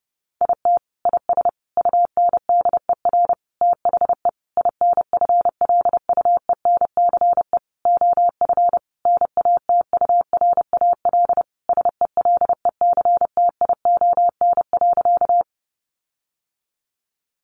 Morse code, heard as a keyed tone, about 30 wpm, 700 Hz. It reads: IT IS UNDER THE INFLUENCE OF NATURAL SELECTION.
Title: Sentences from Top 1400 Words 30wpm